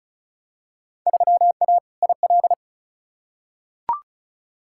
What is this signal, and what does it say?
Morse code, heard as a keyed tone, about 36 wpm, 700 Hz. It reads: 3A IL
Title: Field Day Exchanges - 35wpm